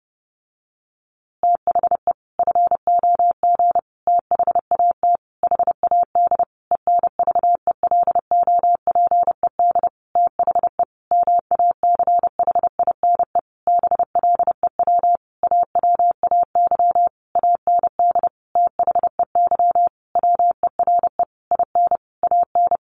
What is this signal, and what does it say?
Morse code, heard as a keyed tone, about 30 wpm, 700 Hz. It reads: THE FOG THAT HAD ENVELOPED THE MACHINE BLEW AWAY AND THEY WERE IN AN
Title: A Sound Of Thunder 30wpm